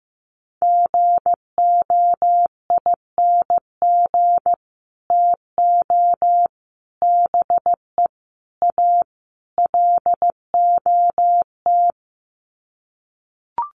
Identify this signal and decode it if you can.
Morse code, heard as a keyed tone, about 15 wpm, 700 Hz. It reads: GOING TO BE A LOT
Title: Sets of 5 Words Encore 15wpm